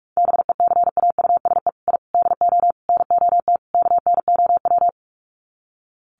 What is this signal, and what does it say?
Morse code, heard as a keyed tone, about 44 wpm, 700 Hz. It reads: BECAUSE I DO NOT KNOW
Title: Sets of 5 Words Encore2 45wpm